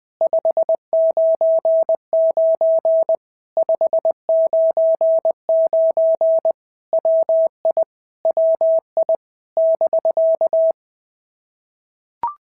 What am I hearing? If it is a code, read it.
Morse code, heard as a keyed tone, about 20 wpm, 660 Hz. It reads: 599 599 WI WI <BK>
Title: POTA QSO Practice - 20wpm